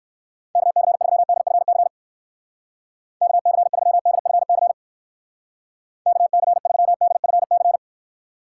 Morse code, heard as a keyed tone, about 64 wpm, 700 Hz. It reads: KC3DFC KC3DFC KC3DFC